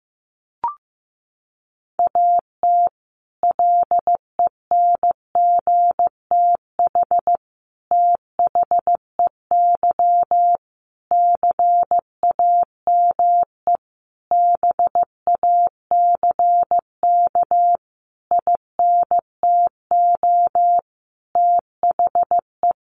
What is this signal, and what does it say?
Morse code, heard as a keyed tone, about 15 wpm, 700 Hz. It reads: AT LENGTH THEY CAME BACK INTO THE
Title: Sentences from Top 1000 Words 15wpm